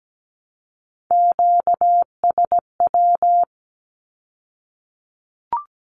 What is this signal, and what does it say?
Morse code, heard as a keyed tone, about 17 wpm, 700 Hz. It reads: QSW
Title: Q-Codes with Spoken Meaning - 17wpm